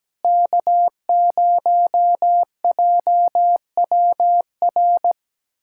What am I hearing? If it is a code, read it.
Morse code, heard as a keyed tone, about 17 wpm, 700 Hz. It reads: K0JWR
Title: USA Callsign DE Callsign - 5 Characters - 17wpm